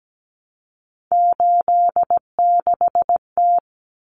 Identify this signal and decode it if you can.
Morse code, heard as a keyed tone, about 17 wpm, 700 Hz. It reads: 86T
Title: Common Cut Numbers - 3-Digit 17wpm